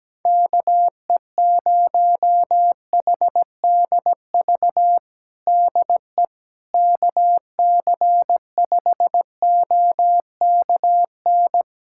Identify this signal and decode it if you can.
Morse code, heard as a keyed tone, about 17 wpm, 700 Hz. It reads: KE0HDV DE KC5OKN
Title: USA Callsign DE Callsign 17wpm